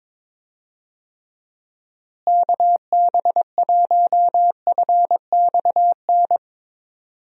Morse code, heard as a keyed tone, about 22 wpm, 700 Hz. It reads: KB1FXN
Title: USA Call Signs - 6 Characters - Repeated 3x - 22wpm